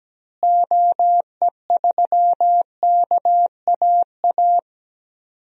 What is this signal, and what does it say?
Morse code, heard as a keyed tone, about 17 wpm, 700 Hz. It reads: OE3KAA
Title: DXCC-100 Call Signs - 6 Characters - Repeated 2x - 17wpm